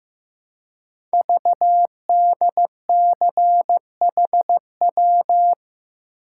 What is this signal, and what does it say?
Morse code, heard as a keyed tone, about 15 wpm, 700 Hz. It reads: VDCHW
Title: Code Groups - 5 Letters - 15wpm